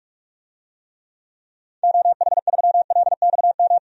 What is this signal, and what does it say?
Morse code, heard as a keyed tone, about 46 wpm, 700 Hz. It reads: OH3LXM
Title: DXCC-100 Call Signs - 6 Characters - 45wpm